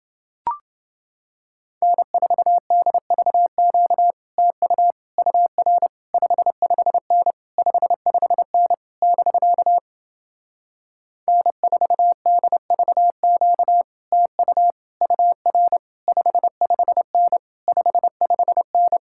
Morse code, heard as a keyed tone, about 30 wpm, 700 Hz. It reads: N4DVQ TU UR 55N 55N <BK> N4DVQ TU UR 55N 55N